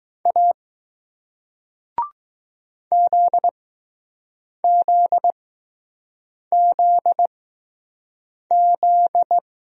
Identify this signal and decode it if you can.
Morse code, heard as a keyed tone, about 20 wpm, spaced as for 6 wpm, 700 Hz. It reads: A ZZZZ